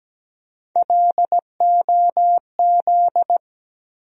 Morse code, heard as a keyed tone, about 17 wpm, 700 Hz. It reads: LOZ
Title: USA Call Sign Postfixes - 3 Letters 17wpm